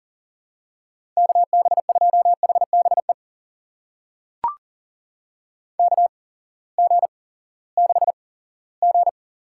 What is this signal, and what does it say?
Morse code, heard as a keyed tone, about 40 wpm, spaced as for 20 wpm, 700 Hz. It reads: KB2HBE KG6G
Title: USA Call Signs - 6 Characters - 8x Character Spacing - 40wpm